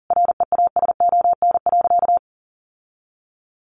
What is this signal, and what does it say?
Morse code, heard as a keyed tone, about 40 wpm, 700 Hz. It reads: REASON.
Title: Sentences from Top 1300 Words 40wpm